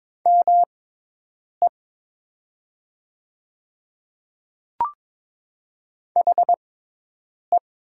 Morse code, heard as a keyed tone, about 22 wpm, spaced as for 8 wpm, 700 Hz. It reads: ME HE
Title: 6x Character Spacing - Top 200 Words - 22wpm